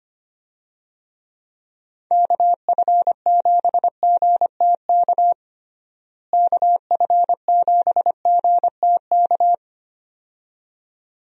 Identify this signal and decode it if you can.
Morse code, heard as a keyed tone, about 25 wpm, 700 Hz. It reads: KF7GTK KF7GTK
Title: USA Call Signs - 6 Characters - Repeated 2x - 25wpm